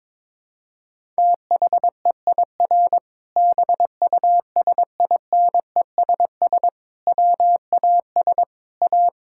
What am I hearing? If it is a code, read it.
Morse code, heard as a keyed tone, about 21 wpm, 700 Hz. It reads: THEIR BUSINESS WAS A
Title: Sentences from Top 1000 Words - Review Part 6 - 22wpm